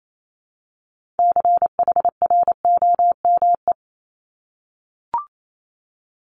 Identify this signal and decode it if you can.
Morse code, heard as a keyed tone, about 28 wpm, 700 Hz. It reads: CHROME
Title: Multi-letter Phonemes Repeated 3x 28wpm